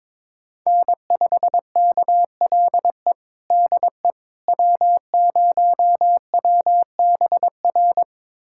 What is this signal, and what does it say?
Morse code, heard as a keyed tone, about 22 wpm, 700 Hz. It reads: N5KLE DE W0WBR
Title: USA Callsign DE Callsign - 5 Characters - 22wpm